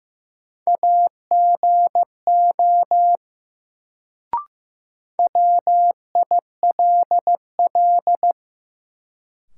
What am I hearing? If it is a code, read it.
Morse code, heard as a keyed tone, about 15 wpm, 700 Hz. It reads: AGO WILL
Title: Sets of 2 Words - New Word Drill - 15wpm